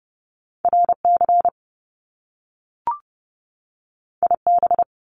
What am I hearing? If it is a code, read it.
Morse code, heard as a keyed tone, about 30 wpm, 700 Hz. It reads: RC IB
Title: Top Two Letter Combinations in Words 30wpm